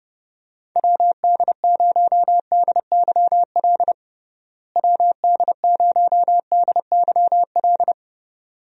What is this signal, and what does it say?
Morse code, heard as a keyed tone, about 29 wpm, 700 Hz. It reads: WD0DYL WD0DYL
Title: USA Call Signs - 6 Characters - Repeated 2x - 30wpm